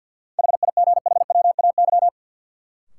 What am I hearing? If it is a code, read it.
Morse code, heard as a keyed tone, about 52 wpm, 700 Hz. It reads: HIGHWAY
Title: Sentences from Top 1400 Words - New Word Drill Alt - 50wpm